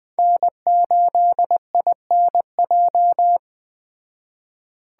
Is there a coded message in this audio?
Yes: Morse code from 0.2 to 3.4 s, heard as a keyed tone, about 20 wpm, 700 Hz. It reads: N8INJ